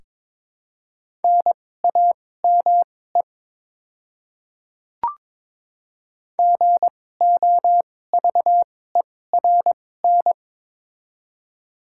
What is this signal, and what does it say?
Morse code, heard as a keyed tone, about 21 wpm, spaced as for 16 wpm, 700 Hz. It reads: NAME GOVERN
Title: 2x Character Spacing - Top 500 Words - 22wpm